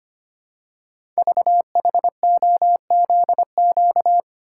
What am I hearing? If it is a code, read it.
Morse code, heard as a keyed tone, about 25 wpm, 700 Hz. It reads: VHOZQ
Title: Code Groups - 5 Letters - 25wpm